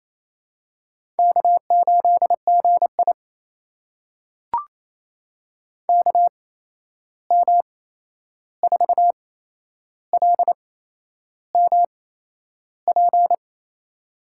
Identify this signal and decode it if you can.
Morse code, heard as a keyed tone, about 28 wpm, spaced as for 11 wpm, 700 Hz. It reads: K8GI KM4LMP
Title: USA Call Signs - 8x Character Spacing - 28wpm